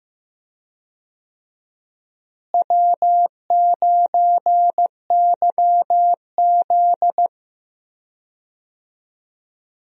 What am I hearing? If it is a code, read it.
Morse code, heard as a keyed tone, about 15 wpm, 700 Hz. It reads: W9YZ
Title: USA Call Signs - 4 Characters - Repeated 3x - 15wpm